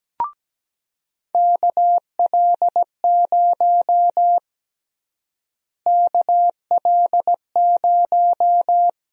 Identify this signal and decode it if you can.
Morse code, heard as a keyed tone, about 17 wpm, 700 Hz. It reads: KL0 KL0
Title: USA Call Sign Prefixes - Repeated 2x - 17wpm